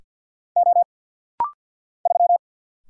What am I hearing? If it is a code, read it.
Morse code, heard as a keyed tone, about 50 wpm, 700 Hz. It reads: O 3
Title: Single Letter-Number - Rapid-Fire 50wpm